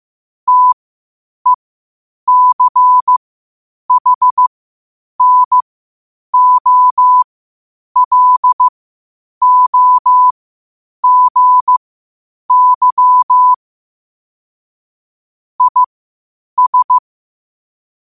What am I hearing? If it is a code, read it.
Morse code, heard as a keyed tone, about 15 wpm, spaced as for 8 wpm, 1000 Hz. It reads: TECHNOLOGY IS
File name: qotdfiles / 8wpm